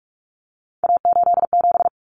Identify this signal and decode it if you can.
Morse code, heard as a keyed tone, about 44 wpm, 700 Hz. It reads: A87
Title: Common Cut Numbers - 3-Digit 45wpm